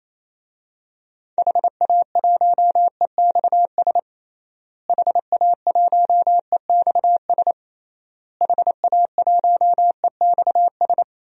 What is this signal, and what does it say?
Morse code, heard as a keyed tone, about 27 wpm, 700 Hz. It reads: HA1EXS HA1EXS HA1EXS
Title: DXCC-100 Call Signs - 6 Characters - Repeated 3x - 28wpm